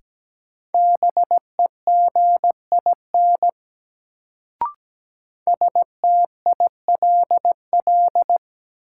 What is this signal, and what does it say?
Morse code, heard as a keyed tone, about 17 wpm, 700 Hz. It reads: BEGIN STILL
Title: Sentences from Top 300 Words - New Word Drill - 17wpm